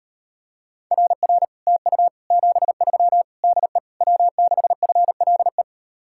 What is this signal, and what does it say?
Morse code, heard as a keyed tone, about 38 wpm, 700 Hz. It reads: RR TU 73 DE W6FLE